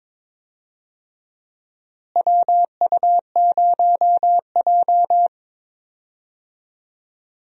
Morse code, heard as a keyed tone, about 22 wpm, 700 Hz. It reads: WU0J
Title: USA Call Signs - 4 Characters - Repeated 3x - 22wpm